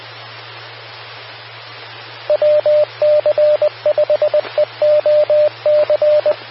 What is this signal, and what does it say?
Morse code, heard as a keyed tone, about 20 wpm, 600 Hz, with radio noise. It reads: WC5EOC
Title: Austin, TX Amateur Repeaters